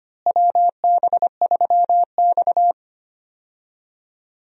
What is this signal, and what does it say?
Morse code, heard as a keyed tone, about 25 wpm, 700 Hz. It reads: WB3X
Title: USA Call Signs - 4 Characters - Repeated 3x - 25wpm